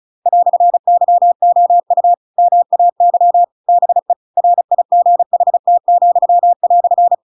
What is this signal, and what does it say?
Morse code, heard as a keyed tone, about 34 wpm, 700 Hz. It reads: "YOU MAY BE RIGHT,"
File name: Pinocchio / Pinocchio ch7 35 35 ik1hge